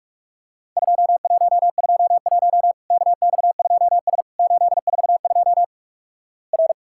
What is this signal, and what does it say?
Morse code, heard as a keyed tone, about 46 wpm, 700 Hz. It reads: 2121 KX2S 842 R
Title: Worked All Europe DX Contest - QTC Exchange - 45wpm